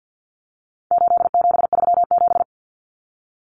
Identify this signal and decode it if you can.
Morse code, heard as a keyed tone, about 50 wpm, 700 Hz. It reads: 8737
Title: Common Cut Numbers - 4-Digit 50wpm